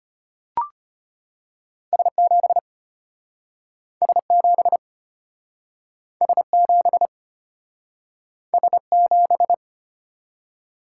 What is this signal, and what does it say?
Morse code, heard as a keyed tone, about 35 wpm, 700 Hz. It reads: S7 S7 S7 S7